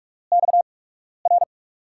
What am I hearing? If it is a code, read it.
Morse code, heard as a keyed tone, about 47 wpm, spaced as for 12 wpm, 700 Hz. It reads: XR
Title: USA Call Signs - 5 Characters - 8x Character Spacing - 45wpm